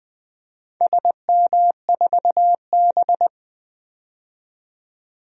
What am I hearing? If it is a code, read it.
Morse code, heard as a keyed tone, about 20 wpm, 700 Hz. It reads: SM4B
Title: DXCC-100 Call Signs - 4 Characters - 20wpm